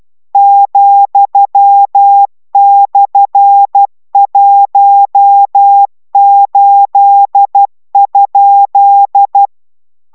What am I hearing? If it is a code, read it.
Morse code, heard as a keyed tone, about 12 wpm, 800 Hz. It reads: ,/18?